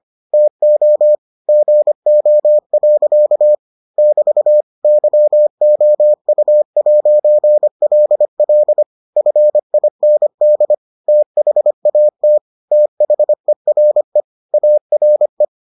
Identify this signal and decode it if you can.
Morse code, heard as a keyed tone, about 25 wpm, 600 Hz. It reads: TO GO. <BT> YOU'LL FIND THAT THERE ARE